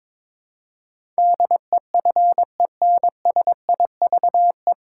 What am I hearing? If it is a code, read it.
Morse code, heard as a keyed tone, about 22 wpm, 700 Hz. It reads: DEFENSIVE
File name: 9 Letter Words - First 3 Letters Given - 22wpm